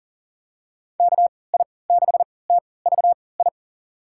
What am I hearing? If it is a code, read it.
Morse code, heard as a keyed tone, about 40 wpm, spaced as for 23 wpm, 700 Hz. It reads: KI6TVI
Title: USA Call Signs - 6 Characters - 3x Character Spacing - 40wpm